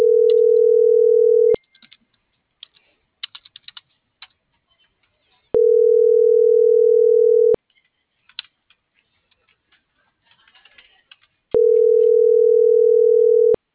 Ringback tone (North America), 440 + 480 Hz.